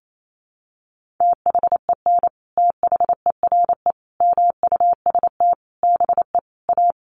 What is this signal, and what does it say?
Morse code, heard as a keyed tone, about 28 wpm, 700 Hz. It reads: THEN THERE MUST BE A
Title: Sentences from Top 1400 Words 28wpm